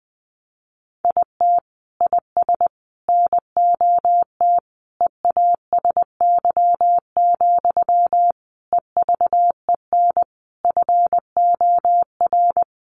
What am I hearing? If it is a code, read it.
Morse code, heard as a keyed tone, about 19 wpm, 700 Hz. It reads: IT IS NOT EASY, EVEN FOR
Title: Sentences from Top 700 Words 20wpm